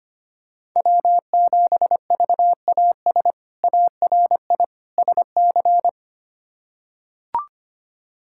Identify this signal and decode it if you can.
Morse code, heard as a keyed tone, about 25 wpm, 700 Hz. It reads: W7VAS ARI SC
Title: K1USN SST QSO Practice - 25wpm